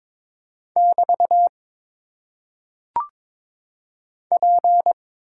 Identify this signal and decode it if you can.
Morse code, heard as a keyed tone, about 22 wpm, 700 Hz. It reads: <BT> P
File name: Top 90 QSO Elements - 8x Character Spacing - 22wpm